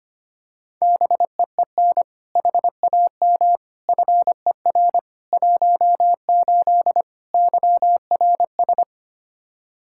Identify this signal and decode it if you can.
Morse code, heard as a keyed tone, about 25 wpm, 700 Hz. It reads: BEEN HAM FER 18 YRS
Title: Practice QSO - Exchange 3 - More Info - 25wpm